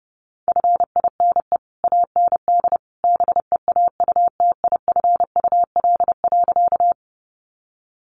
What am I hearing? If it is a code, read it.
Morse code, heard as a keyed tone, about 30 wpm, 700 Hz. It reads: FINE AND BEAUTIFUL.
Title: Sentences from Top 900 Words 30wpm